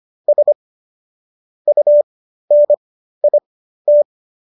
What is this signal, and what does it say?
Morse code, heard as a keyed tone, about 25 wpm, spaced as for 13 wpm, 590 Hz. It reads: S UNIT